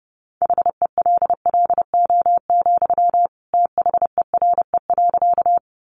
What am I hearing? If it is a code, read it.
Morse code, heard as a keyed tone, about 30 wpm, 700 Hz. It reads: HELLO, THERE.
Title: Sentences from Top 1100 Words 30wpm